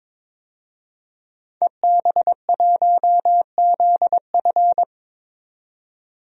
Morse code, heard as a keyed tone, about 22 wpm, 700 Hz. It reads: EB1ZF
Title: DXCC-100 Call Signs - 5 Characters - 22wpm